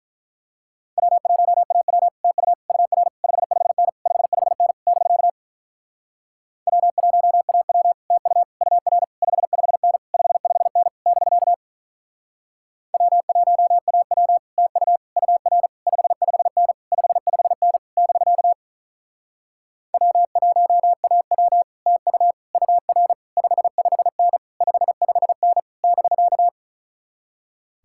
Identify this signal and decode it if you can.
Morse code, heard as a keyed tone, about 45 wpm, 700 Hz. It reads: W1AW TU UR 55N 55N <BK> W1AW TU UR 55N 55N <BK> W1AW TU UR 55N 55N <BK> W1AW TU UR 55N 55N <BK>